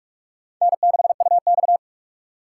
Morse code, heard as a keyed tone, about 44 wpm, 700 Hz. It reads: N6UX
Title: USA Call Signs - 4 Characters - 3x Character Spacing - 45wpm